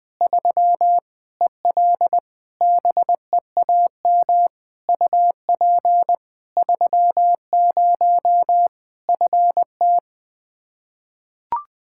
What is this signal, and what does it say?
Morse code, heard as a keyed tone, about 21 wpm, 700 Hz. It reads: 3 EL BEAM UP 30 FT
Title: Practice QSO - Exchange 2 - Info - 20wpm